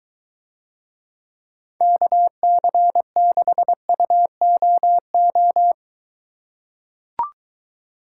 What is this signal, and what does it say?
Morse code, heard as a keyed tone, about 22 wpm, 700 Hz. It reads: KC6UOO